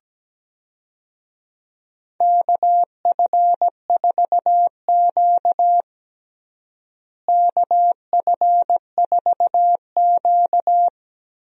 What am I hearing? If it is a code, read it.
Morse code, heard as a keyed tone, about 17 wpm, 700 Hz. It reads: KF4Q KF4Q
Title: USA Call Signs - 4 Characters - Repeated 2x - 17wpm